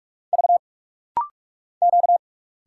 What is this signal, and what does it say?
Morse code, heard as a keyed tone, about 45 wpm, 690 Hz. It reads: V Q
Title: Single Letters - Rapid-Fire 45wpm